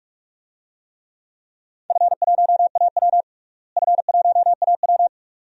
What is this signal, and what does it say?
Morse code, heard as a keyed tone, about 46 wpm, 700 Hz. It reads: F1AW F1AW
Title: DXCC-100 Call Signs - 4 Characters - Repeated 2x - 45wpm